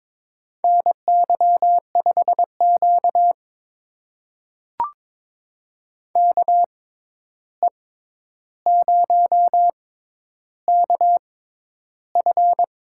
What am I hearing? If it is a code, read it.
Morse code, heard as a keyed tone, about 22 wpm, spaced as for 11 wpm, 700 Hz. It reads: NY5Q KE0KF
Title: USA Call Signs - 6x Character Spacing - 22wpm